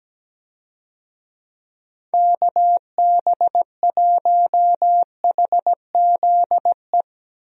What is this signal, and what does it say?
Morse code, heard as a keyed tone, about 17 wpm, 700 Hz. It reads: KB1HZE
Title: USA Call Signs - 6 Characters - 17wpm